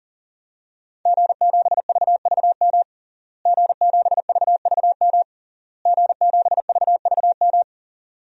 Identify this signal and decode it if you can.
Morse code, heard as a keyed tone, about 40 wpm, 700 Hz. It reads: G7VVM G7VVM G7VVM
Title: DXCC-100 Call Signs - 5 Characters - Repeated 3x - 40wpm